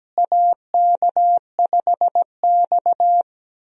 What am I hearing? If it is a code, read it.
Morse code, heard as a keyed tone, about 17 wpm, 700 Hz. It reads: AK5X
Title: USA Call Signs - 4 Characters - Repeated 2x - 17wpm